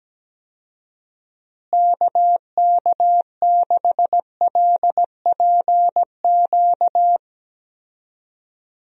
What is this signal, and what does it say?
Morse code, heard as a keyed tone, about 17 wpm, 700 Hz. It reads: KK6LPQ
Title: USA Call Signs - 6 Characters - Repeated 2x - 17wpm